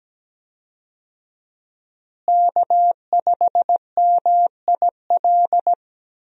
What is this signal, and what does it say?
Morse code, heard as a keyed tone, about 17 wpm, 700 Hz. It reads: K5MIL
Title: USA Call Signs - 5 Characters - Repeated 2x - 17wpm